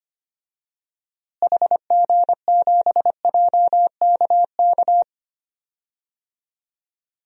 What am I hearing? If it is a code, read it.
Morse code, heard as a keyed tone, about 25 wpm, 700 Hz. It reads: HG7JKK
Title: DXCC-100 Call Signs - 6 Characters - Repeated 3x - 25wpm